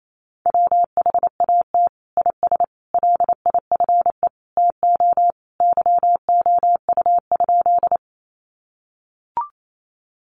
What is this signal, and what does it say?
Morse code, heard as a keyed tone, about 28 wpm, 700 Hz. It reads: WHAT IS LIFE TO YOU?
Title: Sentences from Top 300 Words 28wpm